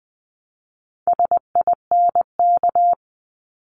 Morse code, heard as a keyed tone, about 20 wpm, 700 Hz. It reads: SINK
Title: 260 Lesson 42e - Words 20-20wpm